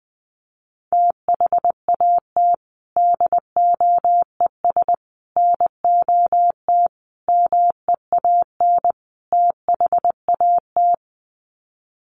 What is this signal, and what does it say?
Morse code, heard as a keyed tone, about 20 wpm, 700 Hz. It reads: THAT DOES NOT MEAN THAT
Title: Sets of 5 Words Encore 20wpm